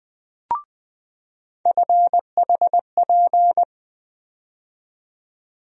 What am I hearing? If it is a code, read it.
Morse code, heard as a keyed tone, about 21 wpm, 700 Hz. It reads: FHP
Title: USA Call Sign Postfixes - 3 Letters 20wpm